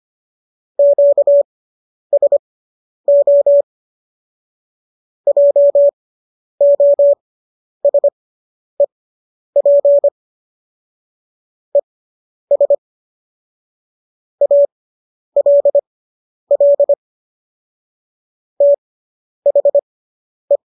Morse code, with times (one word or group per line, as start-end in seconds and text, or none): 0.8-3.6 QSO
5.3-10.1 JOSEP
11.8-12.8 ES
14.4-16.9 ALL
18.6-20.6 THE